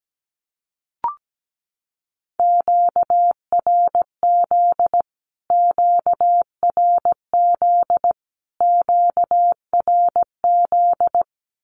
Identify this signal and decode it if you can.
Morse code, heard as a keyed tone, about 17 wpm, 700 Hz. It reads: QRZ QRZ QRZ
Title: Q-Codes Repeated 3x 17wpm